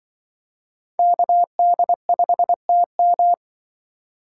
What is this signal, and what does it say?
Morse code, heard as a keyed tone, about 24 wpm, 700 Hz. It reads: KD5TM
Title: Speed Racing - USA Call Signs 20wpm